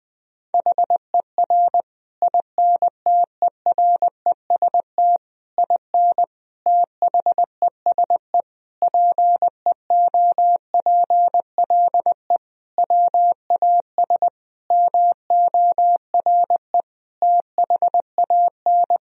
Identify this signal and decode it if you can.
Morse code, heard as a keyed tone, about 20 wpm, 700 Hz. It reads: HER INTEREST IN THESE PEOPLE WAS MORE THAN
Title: Sentences from Top 1000 Words - Review Part 2 - 20wpm